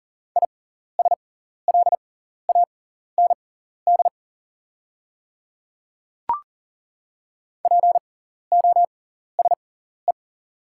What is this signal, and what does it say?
Morse code, heard as a keyed tone, about 40 wpm, spaced as for 14 wpm, 690 Hz. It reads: ISLAND POSE
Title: 6x Character Spacing - Top 500 Words - 40wpm